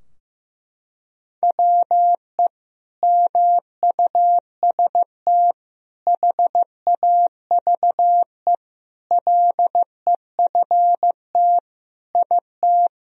Morse code, heard as a keyed tone, about 15 wpm, 700 Hz. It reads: WE MUST HAVE LEFT IT
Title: Sentences from Top 1000 Words - Review Part 2 - 15wpm